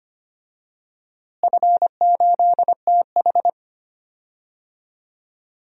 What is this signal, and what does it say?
Morse code, heard as a keyed tone, about 25 wpm, 700 Hz. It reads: F8TH
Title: DXCC-100 Call Signs - 4 Characters - Repeated 2x - 25wpm